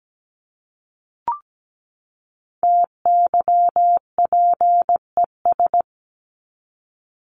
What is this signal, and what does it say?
Morse code, heard as a keyed tone, about 17 wpm, 700 Hz. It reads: TYPES
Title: ES:S Suffix - Top 100 Words with Base Words 17wpm